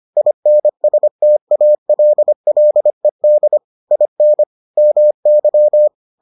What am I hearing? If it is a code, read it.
Morse code, heard as a keyed tone, about 25 wpm, 600 Hz. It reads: INSTALLED IN MY